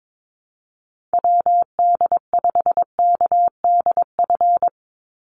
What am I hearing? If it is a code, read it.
Morse code, heard as a keyed tone, about 22 wpm, 700 Hz. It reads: WD5KDF
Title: USA Call Signs 22wpm